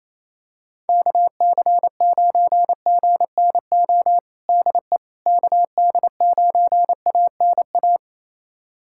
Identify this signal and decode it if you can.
Morse code, heard as a keyed tone, about 27 wpm, 700 Hz. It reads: KC9GNO DE KD9ANA
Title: USA Callsign DE Callsign - 6 Characters - 28wpm